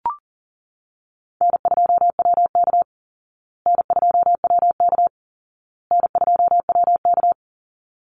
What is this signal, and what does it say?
Morse code, heard as a keyed tone, about 40 wpm, 700 Hz. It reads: N2WK N2WK N2WK
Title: Call Signs Repeated 2x - 40wpm